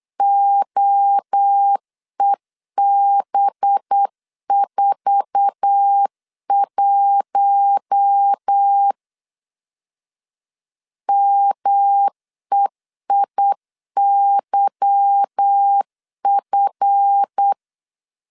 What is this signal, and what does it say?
Morse code, heard as a keyed tone, about 8 wpm, 800 Hz. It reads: OEB41 MEIYF
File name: LEZIONI / 8 YJ45 / 8 YJ4 8wpm 3246